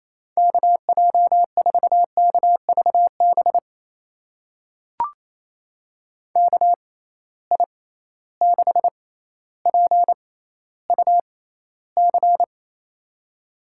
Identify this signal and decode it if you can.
Morse code, heard as a keyed tone, about 27 wpm, spaced as for 14 wpm, 700 Hz. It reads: KJ4KVB KI6PUC
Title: USA Call Signs - 6 Characters - 6x Character Spacing - 28wpm